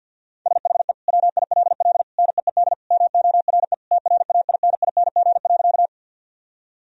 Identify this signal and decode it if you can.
Morse code, heard as a keyed tone, about 50 wpm, 700 Hz. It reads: SHE WILL NEED MORE TRAINING.
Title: Sentences from Top 1000 Words - Review Part 4 - 50wpm